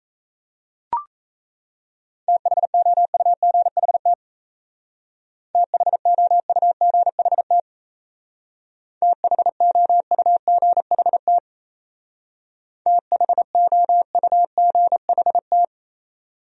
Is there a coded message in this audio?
Yes: Morse code from 2.3 to 15.6 s, heard as a keyed tone, about 36 wpm, 700 Hz. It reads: THOUGHT THOUGHT THOUGHT THOUGHT